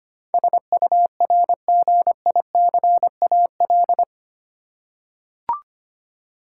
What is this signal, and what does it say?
Morse code, heard as a keyed tone, about 25 wpm, 700 Hz. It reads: SURGICAL
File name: Eight Letter Words - 25wpm